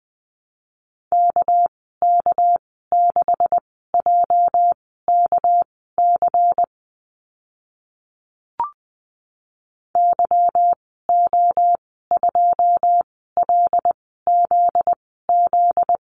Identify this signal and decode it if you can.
Morse code, heard as a keyed tone, about 20 wpm, spaced as for 14 wpm, 700 Hz. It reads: KK6JKC YO2LZZ
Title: 245 Lesson 39g - Letter z - Call Signs 20wpm